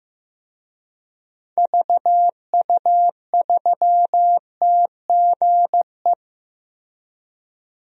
Morse code, heard as a keyed tone, about 15 wpm, 700 Hz. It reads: VU3TGE
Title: DXCC-100 Call Signs - 6 Characters - Repeated 2x - 15wpm